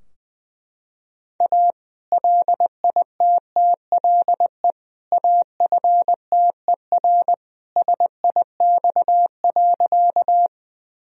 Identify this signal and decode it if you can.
Morse code, heard as a keyed tone, about 20 wpm, 700 Hz. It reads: A LITTLE AFTER SIX.
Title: Sentences from Top 1000 Words - Review Part 2 - 20wpm